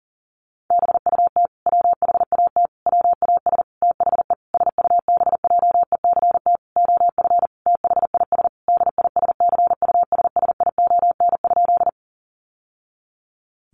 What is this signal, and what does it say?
Morse code, heard as a keyed tone, about 40 wpm, 700 Hz. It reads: BUT WHAT WAS THE SUBJECT OF THIS DISCUSSION?
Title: Sentences from Top 1100 Words 40wpm